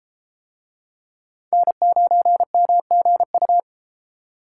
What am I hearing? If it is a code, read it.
Morse code, heard as a keyed tone, about 31 wpm, 700 Hz. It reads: N9MGU